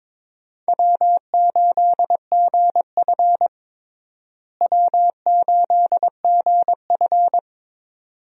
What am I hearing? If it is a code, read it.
Morse code, heard as a keyed tone, about 22 wpm, 700 Hz. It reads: W8GF W8GF
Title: USA Call Signs - 4 Characters - Repeated 2x - 22wpm